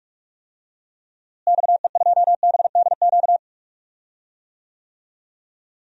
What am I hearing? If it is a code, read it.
Morse code, heard as a keyed tone, about 47 wpm, 700 Hz. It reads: XE2BDQ
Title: DXCC-100 Call Signs - 6 Characters - Repeated 2x - 45wpm